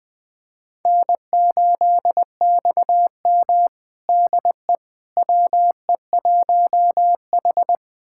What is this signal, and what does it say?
Morse code, heard as a keyed tone, about 20 wpm, 700 Hz. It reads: N8XM DE WE1H
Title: USA Callsign DE Callsign - 4 Characters - 20wpm